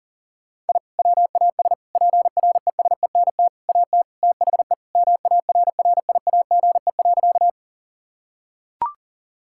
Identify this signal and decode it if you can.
Morse code, heard as a keyed tone, about 40 wpm, 700 Hz. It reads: I WAS PRESENT AT THE MARRIAGE.
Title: Sentences from Top 1000 Words - Review Part 5 - 40wpm